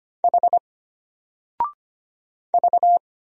Morse code, heard as a keyed tone, about 25 wpm, 700 Hz. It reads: H V
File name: Single-Letter-Number-25wpm